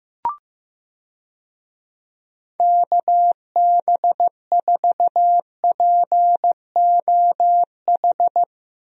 Morse code, heard as a keyed tone, about 15 wpm, 700 Hz. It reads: KB4POH
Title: USA Call Signs - 6 Characters - Repeated 3x - 15wpm